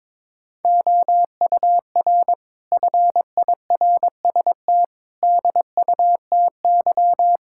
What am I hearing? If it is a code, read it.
Morse code, heard as a keyed tone, about 22 wpm, 700 Hz. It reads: OUR FIRST DUTY
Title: Sentences from Top 1600 Words 22wpm